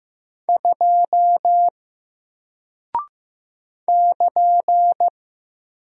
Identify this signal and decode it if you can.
Morse code, heard as a keyed tone, about 15 wpm, 700 Hz. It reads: 2 <KN>
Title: Single Character (Letter:Number:Punctuation:Prosign) 15wpm